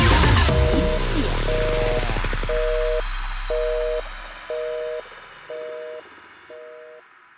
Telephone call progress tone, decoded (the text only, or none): busy tone